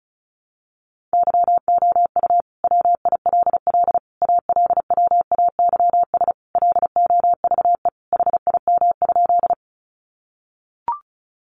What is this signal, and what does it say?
Morse code, heard as a keyed tone, about 34 wpm, 700 Hz. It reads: YOU WILL ALWAYS LOVE HIM?
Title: Sentences from Top 400 Words 35wpm